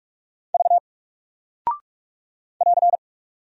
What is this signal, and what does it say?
Morse code, heard as a keyed tone, about 46 wpm, 700 Hz. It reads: V <AR>
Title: Single Character (Letter:Number:Punctuation:Prosign) Alt 45wpm